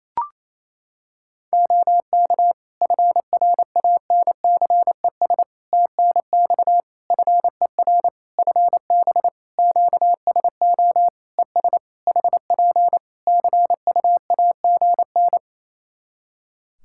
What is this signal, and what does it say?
Morse code, heard as a keyed tone, about 28 wpm, 700 Hz. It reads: OK FRANCES TNX FER FB QSO ES HP CUAGN
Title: Practice QSO - Exchange 4 - Ending - 28wpm